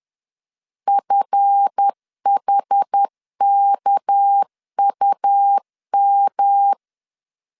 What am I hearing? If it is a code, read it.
Morse code, heard as a keyed tone, about 10 wpm, 800 Hz. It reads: FHKUM
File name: LEZIONI / 8 YJ45 / 8 YJ45 10wpm 2549